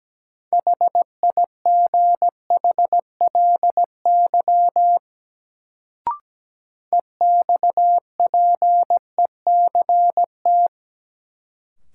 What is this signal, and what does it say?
Morse code, heard as a keyed tone, about 17 wpm, 700 Hz. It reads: HIGHLY EXPECT
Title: Sentences from Top 1100 Words - New Word Drill - 17wpm